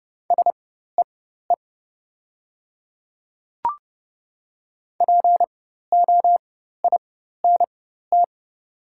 Morse code, heard as a keyed tone, about 30 wpm, spaced as for 14 wpm, 700 Hz. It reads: SEE POINT
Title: 4x Character Spacing - Top 300 Words - 30wpm